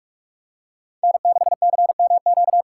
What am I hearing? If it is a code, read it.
Morse code, heard as a keyed tone, about 41 wpm, 700 Hz. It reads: N6CMQ